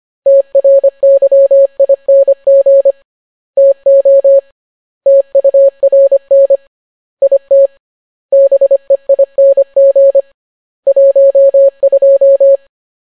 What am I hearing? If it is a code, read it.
Morse code, heard as a keyed tone, about 25 wpm, 550 Hz. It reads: TRYING TO TURN IT BEING 12